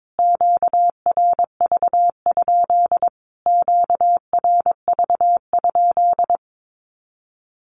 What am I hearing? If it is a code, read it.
Morse code, heard as a keyed tone, about 22 wpm, 700 Hz. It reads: QRV? QRV?
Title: Q-Codes Repeated 3x 22wpm